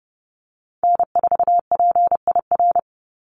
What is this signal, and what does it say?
Morse code, heard as a keyed tone, about 30 wpm, 700 Hz. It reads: N4PIR
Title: Call Signs Repeated 3x - 30wpm